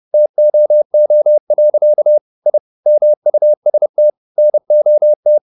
Morse code, heard as a keyed tone, about 30 wpm, 600 Hz. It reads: TOO. I MUST NOT